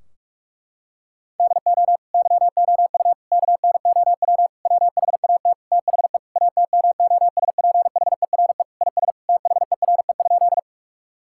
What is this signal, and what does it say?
Morse code, heard as a keyed tone, about 46 wpm, 700 Hz. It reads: DO YOU KNOW WHAT THE ATMOSPHERE IS THERE?
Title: Sentences from Top 1600 Words 45wpm